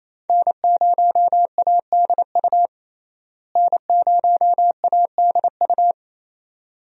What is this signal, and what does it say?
Morse code, heard as a keyed tone, about 28 wpm, 700 Hz. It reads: N0ADU N0ADU
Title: USA Call Signs - 5 Characters - Repeated 2x - 28wpm